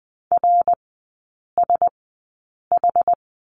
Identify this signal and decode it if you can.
Morse code, heard as a keyed tone, about 20 wpm, spaced as for 8 wpm, 690 Hz. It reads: RSH
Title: 081 Lesson 14e - Number 2 - Three Characters 20wpm